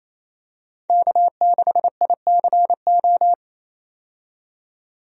Morse code, heard as a keyed tone, about 28 wpm, 700 Hz. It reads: K6ICO
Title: USA Call Signs - 5 Characters - 28wpm